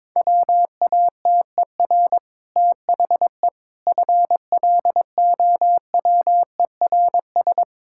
Morse code, heard as a keyed tone, about 23 wpm, 700 Hz. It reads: WATER THE FLOWERS
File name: practice / 22wpm / Easy Sentences with Hints - 22wpm